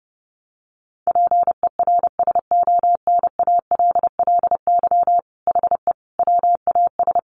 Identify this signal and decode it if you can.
Morse code, heard as a keyed tone, about 30 wpm, 700 Hz. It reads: PERSONALLY HE WAS
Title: Sentences from Top 1400 Words 30wpm